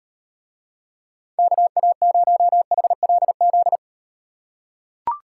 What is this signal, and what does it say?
Morse code, heard as a keyed tone, about 38 wpm, 700 Hz. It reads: KA0HLZ